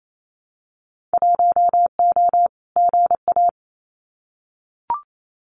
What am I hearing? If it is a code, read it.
Morse code, heard as a keyed tone, about 28 wpm, 700 Hz. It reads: 1O GA
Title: Winter Field Day Exchanges 28wpm